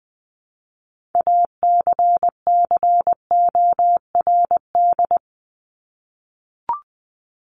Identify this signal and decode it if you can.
Morse code, heard as a keyed tone, about 20 wpm, 700 Hz. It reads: ACCORD
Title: ING Suffix - Top 100 Words with Base Words 20wpm